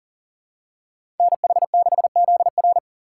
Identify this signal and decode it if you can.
Morse code, heard as a keyed tone, about 40 wpm, 700 Hz. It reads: NH6ZR
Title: USA Call Signs - 5 Characters - 2x Character Spacing - 40wpm